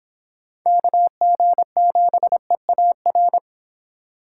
Morse code, heard as a keyed tone, about 25 wpm, 700 Hz. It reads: KG7EAR